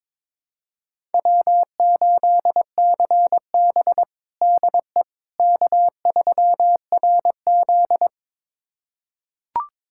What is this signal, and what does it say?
Morse code, heard as a keyed tone, about 22 wpm, 700 Hz. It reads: W8CB DE K3RZ
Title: USA Callsign DE Callsign - 4 Characters - 22wpm